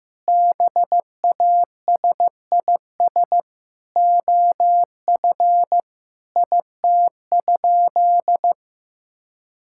Morse code, heard as a keyed tone, about 15 wpm, 700 Hz. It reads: BASIS OF IT?
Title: Sentences from Top 1000 Words - Review Part 5 - 15wpm